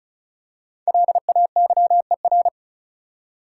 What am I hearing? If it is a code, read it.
Morse code, heard as a keyed tone, about 34 wpm, 700 Hz. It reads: LAYER
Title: Five Letter Words - 35wpm